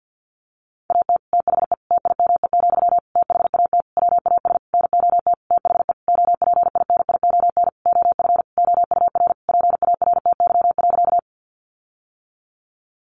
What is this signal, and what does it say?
Morse code, heard as a keyed tone, about 50 wpm, 700 Hz. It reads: AT THE TIME, THAT WAS NOT THE OPINION OF OUR PARTY.
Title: Sentences from Top 1400 Words 50wpm